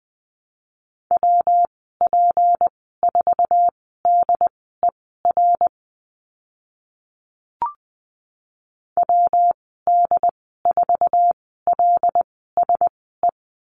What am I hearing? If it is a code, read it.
Morse code, heard as a keyed tone, about 20 wpm, spaced as for 15 wpm, 700 Hz. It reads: WP4DER WD4LSE
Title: 161 Lesson 26g - Letter g - Call Signs 20wpm